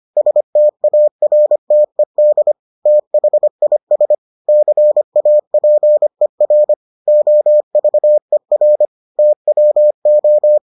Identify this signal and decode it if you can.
Morse code, heard as a keyed tone, about 25 wpm, 600 Hz. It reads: STARTED THIS CAPER OVER TWO